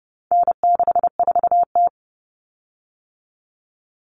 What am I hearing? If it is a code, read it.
Morse code, heard as a keyed tone, about 30 wpm, 700 Hz. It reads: N64T
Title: Common Cut Numbers - 4-Digit 30wpm